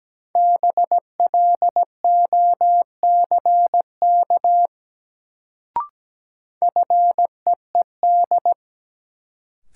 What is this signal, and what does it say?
Morse code, heard as a keyed tone, about 17 wpm, 700 Hz. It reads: BLOCK FEED
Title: Sentences from Top 1600 Words - New Word Drill - 17wpm